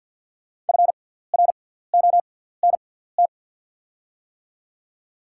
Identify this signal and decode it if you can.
Morse code, heard as a keyed tone, about 53 wpm, spaced as for 17 wpm, 690 Hz. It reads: FRONT
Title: 6x Character Spacing - Top 500 Words - 50wpm